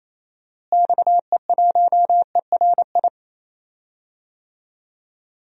Morse code, heard as a keyed tone, about 27 wpm, 700 Hz. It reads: XE1ERI
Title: DXCC-100 Call Signs - 6 Characters - Repeated 3x - 28wpm